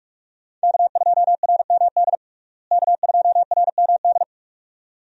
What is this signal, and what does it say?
Morse code, heard as a keyed tone, about 46 wpm, 700 Hz. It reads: K2RMD K2RMD
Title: USA Call Signs - 5 Characters - Repeated 2x - 45wpm